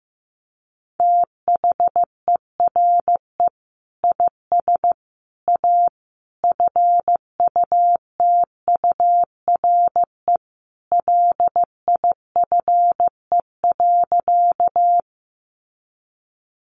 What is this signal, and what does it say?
Morse code, heard as a keyed tone, about 15 wpm, 700 Hz. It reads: THERE IS A FUTURE LIFE.
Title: Sentences from Top 700 Words 15wpm